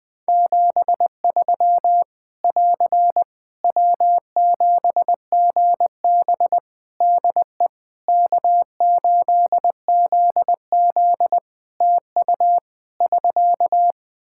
Morse code, heard as a keyed tone, about 20 wpm, 700 Hz. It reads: 73 <AR> W7GB DE K8ZZ TU <SK>
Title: Practice QSO - Exchange 4 - Ending - 20wpm